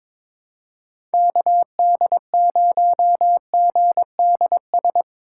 Morse code, heard as a keyed tone, about 22 wpm, 700 Hz. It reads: KD0GDS
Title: USA Call Signs - 6 Characters - 3x Character Spacing - 22wpm